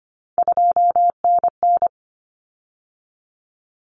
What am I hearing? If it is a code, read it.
Morse code, heard as a keyed tone, about 25 wpm, 700 Hz. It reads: 2NN
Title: Common Cut Numbers - 3-Digit 25wpm